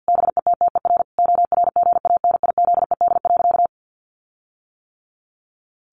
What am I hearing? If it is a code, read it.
Morse code, heard as a keyed tone, about 50 wpm, 700 Hz. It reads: BETTER ORGANIZED.
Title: Sentences from Top 1300 Words 50wpm